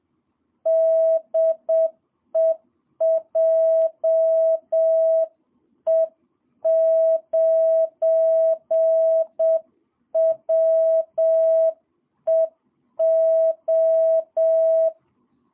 Morse code, heard as a keyed tone, about 7 wpm, 660 Hz. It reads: DE JE9WEO
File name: jh8cbh / 2022cw